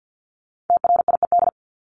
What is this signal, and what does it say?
Morse code, heard as a keyed tone, about 50 wpm, 700 Hz. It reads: TRIED
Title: ED Suffix - Top 100 Words with Base Words 50wpm